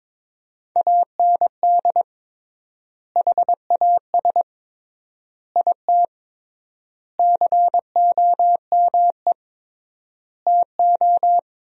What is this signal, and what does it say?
Morse code, heard as a keyed tone, about 22 wpm, 700 Hz. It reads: AND HAS IT COME TO
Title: 3x Word-Spacing - Sentences from Top 100 Words - 22wpm